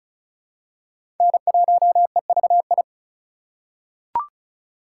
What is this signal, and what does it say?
Morse code, heard as a keyed tone, about 35 wpm, 690 Hz. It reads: N1EVI
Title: USA Call Signs - 5 Characters - 2x Character Spacing - 35wpm